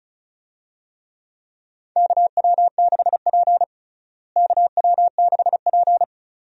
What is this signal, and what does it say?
Morse code, heard as a keyed tone, about 36 wpm, 700 Hz. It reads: KW6P KW6P
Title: USA Call Signs - 4 Characters - Repeated 2x - 35wpm